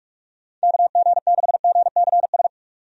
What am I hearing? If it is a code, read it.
Morse code, heard as a keyed tone, about 42 wpm, 700 Hz. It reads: KG6GCS
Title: USA Call Signs - 6 Characters - 45wpm